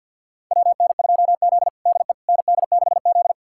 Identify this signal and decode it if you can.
Morse code, heard as a keyed tone, about 50 wpm, 700 Hz. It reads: WN2Z DE ND6Z
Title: USA Callsign DE Callsign - 4 Characters - 50wpm